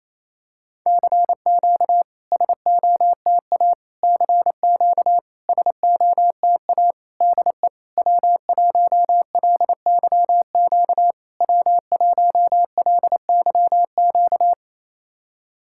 Morse code, heard as a keyed tone, about 28 wpm, 690 Hz. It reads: CQ SOTA CQ SOTA DE W1LYQ W1LYQ
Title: SOTA QSO Practice 28wpm